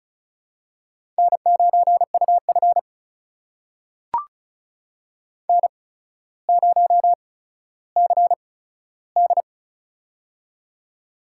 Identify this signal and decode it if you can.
Morse code, heard as a keyed tone, about 36 wpm, spaced as for 15 wpm, 700 Hz. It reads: N9UF N0CD
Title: USA Call Signs - 4 Characters - 8x Character Spacing - 35wpm